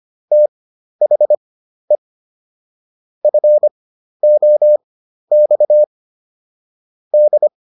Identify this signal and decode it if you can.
Morse code, heard as a keyed tone, about 25 wpm, spaced as for 12 wpm, 600 Hz. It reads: THE FOX D